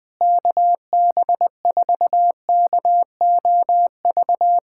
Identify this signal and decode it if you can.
Morse code, heard as a keyed tone, about 20 wpm, 700 Hz. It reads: KB4KOV
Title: USA Callsign DE Callsign - 6 Characters - 20wpm